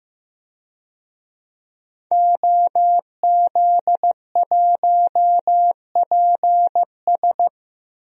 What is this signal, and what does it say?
Morse code, heard as a keyed tone, about 15 wpm, 700 Hz. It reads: OZ1PS
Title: DXCC-100 Call Signs - 5 Characters - 15wpm